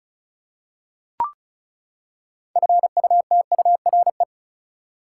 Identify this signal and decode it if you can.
Morse code, heard as a keyed tone, about 36 wpm, 700 Hz. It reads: FUTURE
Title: Six Letter Words - 35wpm